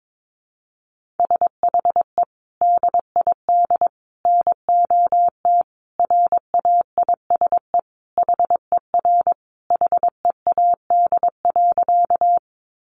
Morse code, heard as a keyed tone, about 22 wpm, 700 Hz. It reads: SHE DID NOT RAISE HER HEAD.
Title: Sentences from Top 1400 Words 22wpm